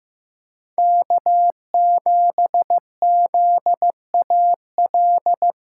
Morse code, heard as a keyed tone, about 15 wpm, 700 Hz. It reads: K7ZAL
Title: USA Callsign DE Callsign - 5 Characters - 15wpm